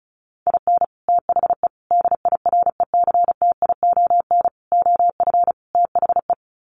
Morse code, heard as a keyed tone, about 34 wpm, 700 Hz. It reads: IN THE DIRECTION OF THE
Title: Sets of 5 Words 35wpm